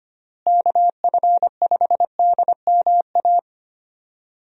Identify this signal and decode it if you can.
Morse code, heard as a keyed tone, about 25 wpm, 700 Hz. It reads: KF5DMA
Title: USA Call Signs - 6 Characters - Repeated 3x - 25wpm